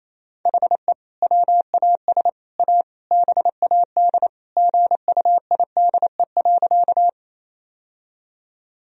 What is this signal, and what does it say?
Morse code, heard as a keyed tone, about 28 wpm, 700 Hz. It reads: HE WAS A BAD GUIDE.
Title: Sentences from Top 1000 Words - Review Part 3 - 28wpm